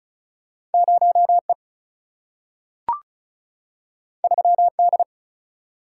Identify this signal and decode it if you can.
Morse code, heard as a keyed tone, about 36 wpm, 700 Hz. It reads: 0 E 3 D
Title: Single-Letter Single-Number 35wpm